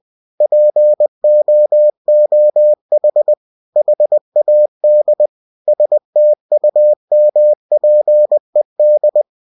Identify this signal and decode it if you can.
Morse code, heard as a keyed tone, about 20 wpm, 600 Hz. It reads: POOH HAD STUMPED